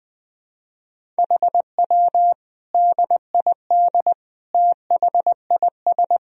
Morse code, heard as a keyed tone, about 21 wpm, 700 Hz. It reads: HW DID THIS
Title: Sentences from Top 400 Words and Abbreviations 20wpm